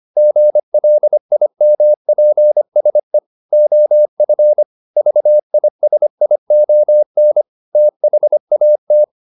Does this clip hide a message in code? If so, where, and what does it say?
Yes: Morse code from 0.2 to 9.0 s, heard as a keyed tone, about 25 wpm, 600 Hz. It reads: GLIMPSE OF VISION THAT